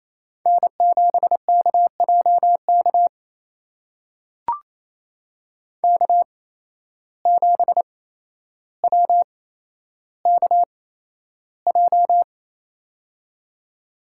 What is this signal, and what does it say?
Morse code, heard as a keyed tone, about 27 wpm, spaced as for 12 wpm, 700 Hz. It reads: N7KJK K7WKJ
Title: USA Call Signs - 5 Characters - 8x Character Spacing - 28wpm